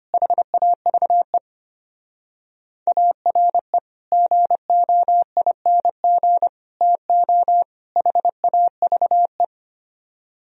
Morse code, heard as a keyed tone, about 32 wpm, 700 Hz. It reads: HAVE ARE GOING TO HAVE